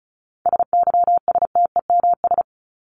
Morse code, heard as a keyed tone, about 34 wpm, 700 Hz. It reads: SYSTEMS
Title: ES:S Suffix - Top 100 Words with Base Words 35wpm